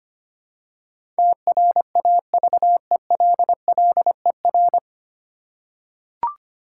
Morse code, heard as a keyed tone, about 25 wpm, 700 Hz. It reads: TRAVELLER
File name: practice / 25wpm / 9 Letter Words - First 3 Letters Given - 25wpm